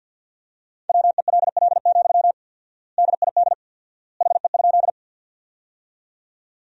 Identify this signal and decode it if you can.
Morse code, heard as a keyed tone, about 50 wpm, 700 Hz. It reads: WELL, DID HE?
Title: 4x Word-Spacing - Sentences from Top 200 Words - 50wpm